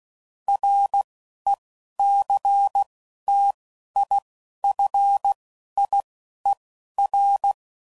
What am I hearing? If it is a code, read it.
Morse code, heard as a keyed tone, about 16 wpm, spaced as for 12 wpm, 800 Hz. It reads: RECTIFIER
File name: CW Code Groups @10WPM / Technical Subjects / SwPwrSup